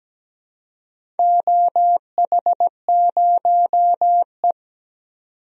Morse code, heard as a keyed tone, about 17 wpm, 700 Hz. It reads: OH0E
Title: DXCC-100 Call Signs - 4 Characters - 17wpm